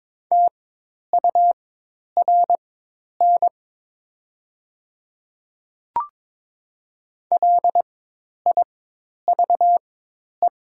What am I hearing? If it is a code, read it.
Morse code, heard as a keyed tone, about 22 wpm, spaced as for 10 wpm, 700 Hz. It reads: TURN LIVE
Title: 4x Character Spacing - Top 200 Words - 22wpm